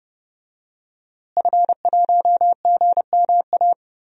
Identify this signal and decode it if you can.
Morse code, heard as a keyed tone, about 30 wpm, 700 Hz. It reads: F1GMA
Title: DXCC-100 Call Signs - 5 Characters - Repeated 3x - 30wpm